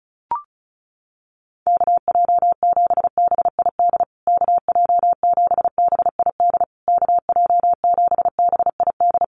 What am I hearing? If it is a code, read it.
Morse code, heard as a keyed tone, about 34 wpm, 700 Hz. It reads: KJ7BID KJ7BID KJ7BID
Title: USA Call Signs Repeated 3x 35wpm